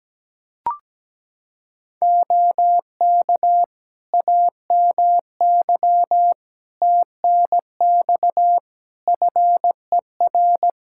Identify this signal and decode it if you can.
Morse code, heard as a keyed tone, about 17 wpm, 700 Hz. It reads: OK AMY TNX FER
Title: Practice QSO - Exchange 4 - Ending - 17wpm